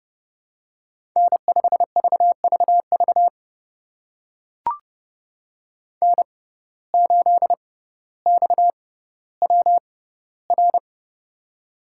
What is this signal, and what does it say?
Morse code, heard as a keyed tone, about 30 wpm, spaced as for 15 wpm, 700 Hz. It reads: N5VVV N8XWR
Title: USA Call Signs - 5 Characters - 6x Character Spacing - 30wpm